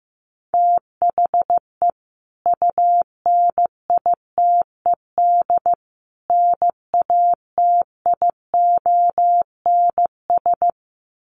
Morse code, heard as a keyed tone, about 15 wpm, 700 Hz. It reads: THE UNITED NATIONS
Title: Sets of 3 Words 15wpm